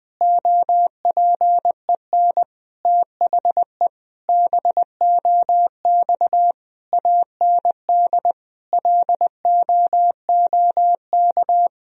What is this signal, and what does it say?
Morse code, heard as a keyed tone, about 21 wpm, 700 Hz. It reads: OPEN THE BOX AND LOOK
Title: Sentences from Top 1000 Words - Review Part 2 - 20wpm